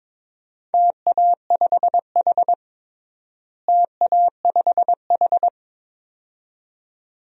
Morse code, heard as a keyed tone, about 22 wpm, 700 Hz. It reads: TA5H TA5H
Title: DXCC-100 Call Signs - 4 Characters - Repeated 2x - 22wpm